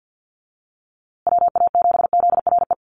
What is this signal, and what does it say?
Morse code, heard as a keyed tone, about 50 wpm, 700 Hz. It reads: WA7ZRE
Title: USA Call Signs Repeated 2x 50wpm